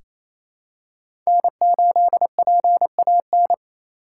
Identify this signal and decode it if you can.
Morse code, heard as a keyed tone, about 28 wpm, 700 Hz. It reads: N8PAN
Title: USA Call Signs - 5 Characters - 8x Character Spacing - 28wpm